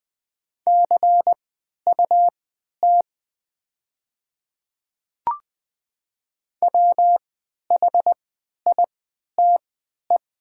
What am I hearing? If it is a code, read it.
Morse code, heard as a keyed tone, about 20 wpm, spaced as for 11 wpm, 700 Hz. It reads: CUT WHITE
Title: 3x Character Spacing - Top 300 Words - 20wpm